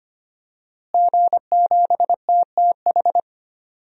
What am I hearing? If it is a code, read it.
Morse code, heard as a keyed tone, about 25 wpm, 700 Hz. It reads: G7TTH
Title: DXCC-100 Call Signs - 5 Characters - Repeated 3x - 25wpm